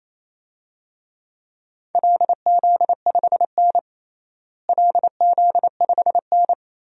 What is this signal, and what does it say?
Morse code, heard as a keyed tone, about 28 wpm, 700 Hz. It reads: LZ5N LZ5N
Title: DXCC-100 Call Signs - 4 Characters - Repeated 2x - 28wpm